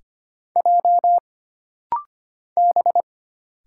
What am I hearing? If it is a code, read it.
Morse code, heard as a keyed tone, about 25 wpm, 700 Hz. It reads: J B
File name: Single Character (Letter:Number:Punctuation:Prosign) - Rapid-Fire - 25wpm